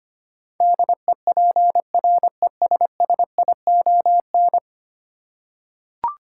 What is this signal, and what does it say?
Morse code, heard as a keyed tone, about 25 wpm, 690 Hz. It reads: DEPRESSION
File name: 10 Letter Words - First 3 Letters Given - 25wpm